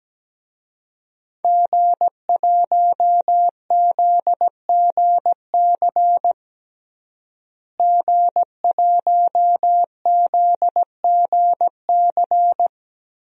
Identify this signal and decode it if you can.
Morse code, heard as a keyed tone, about 17 wpm, 700 Hz. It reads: G1ZGC G1ZGC
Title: DXCC-100 Call Signs - 5 Characters - Repeated 3x - 17wpm